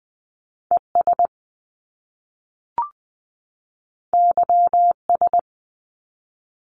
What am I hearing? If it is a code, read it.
Morse code, heard as a keyed tone, about 20 wpm, 700 Hz. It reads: ES YS
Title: Top Two Letter Combinations in Words 20wpm